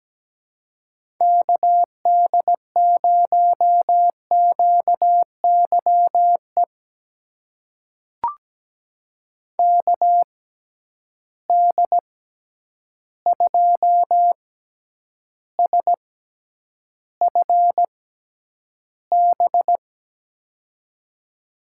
Morse code, heard as a keyed tone, about 17 wpm, spaced as for 9 wpm, 700 Hz. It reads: KD0QYE KD2SFB
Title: USA Call Signs - 6 Characters - 6x Character Spacing - 17wpm